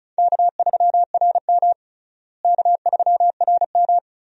Morse code, heard as a keyed tone, about 34 wpm, 700 Hz. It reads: K3RM K3RM
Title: USA Call Signs - 4 Characters - Repeated 2x - 35wpm